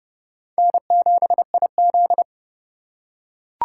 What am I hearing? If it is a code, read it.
Morse code, heard as a keyed tone, about 30 wpm, 700 Hz. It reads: N7IZ